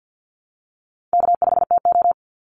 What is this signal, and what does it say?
Morse code, heard as a keyed tone, about 50 wpm, 700 Hz. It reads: K5TO
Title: Easy Call Signs - 50wpm